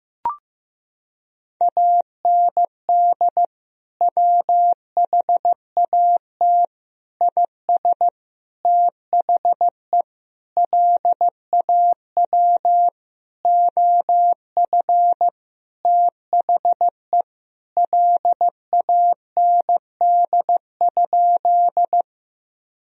Morse code, heard as a keyed tone, about 15 wpm, 700 Hz. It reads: AND WHAT IS THE LAW OF THE LAND?
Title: Sentences from Top 1000 Words - Review Part 3 - 15wpm